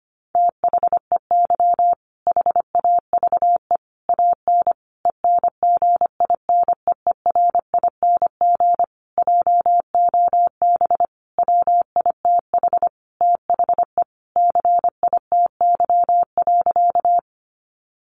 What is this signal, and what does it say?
Morse code, heard as a keyed tone, about 25 wpm, 700 Hz. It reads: THEY HAVE AN ENGINEERING JOB WITH THE CITY.
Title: Sentences from Top 1500 Words 25wpm